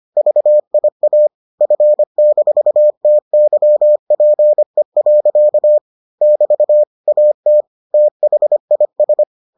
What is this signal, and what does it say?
Morse code, heard as a keyed tone, about 25 wpm, 600 Hz. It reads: VIA F-TYPE. <BT> AT THIS